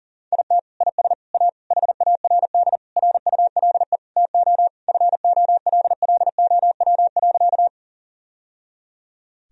Morse code, heard as a keyed tone, about 40 wpm, 700 Hz. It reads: IT IS A HARD RULE TO FOLLOW.
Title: Sentences from Top 1000 Words - Review Part 2 - 40wpm